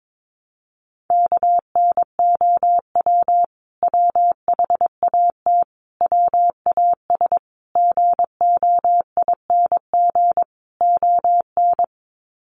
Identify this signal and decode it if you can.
Morse code, heard as a keyed tone, about 22 wpm, 700 Hz. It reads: KNOW WHAT WAS GOING ON
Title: Sets of 5 Words Encore2 22wpm